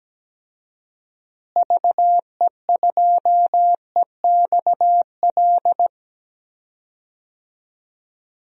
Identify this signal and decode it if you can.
Morse code, heard as a keyed tone, about 17 wpm, 700 Hz. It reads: VE2EXL
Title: DXCC-100 Call Signs - 6 Characters - Repeated 3x - 17wpm